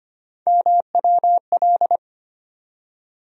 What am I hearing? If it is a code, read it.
Morse code, heard as a keyed tone, about 25 wpm, 700 Hz. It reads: MWL
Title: USA Call Sign Postfixes - 3 Letters 25wpm